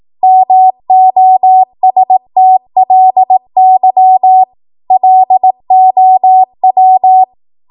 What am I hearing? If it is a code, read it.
Morse code, heard as a keyed tone, about 18 wpm, 750 Hz. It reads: MOSTLY LOW
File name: Morse / 18WPM